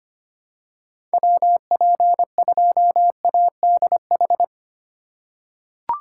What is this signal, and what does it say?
Morse code, heard as a keyed tone, about 25 wpm, 700 Hz. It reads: WP2ADH
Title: USA Call Signs - 6 Characters - 4x Character Spacing - 25wpm